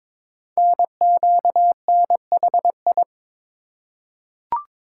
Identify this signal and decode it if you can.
Morse code, heard as a keyed tone, about 22 wpm, 700 Hz. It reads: NQNHI
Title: Code Groups - 5 Letters - 22wpm